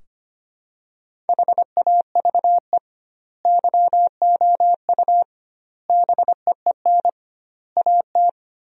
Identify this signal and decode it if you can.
Morse code, heard as a keyed tone, about 25 wpm, 700 Hz. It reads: HAVE YOU BEEN AT
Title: 2x Word-Spacing - Sentences from Top 300 Words - 25wpm